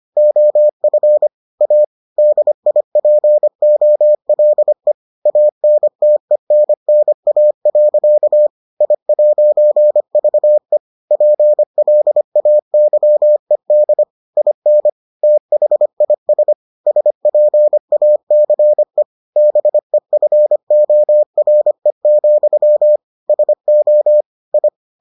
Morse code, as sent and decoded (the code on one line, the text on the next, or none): --- ..-. / .- / -.. .. .--. --- .-.. / .- -. - . -. -. .- .-.-.- / .. .----. ...- . / .--. .-.. .- -.-- . -.. / .. -. / - .... .. ... / ... .--. .- -.-. / -... ..-. --- .-. --..-- / ... --- / ..
OF A DIPOLE ANTENNA. I'VE PLAYED IN THIS SPACE BEFORE, SO I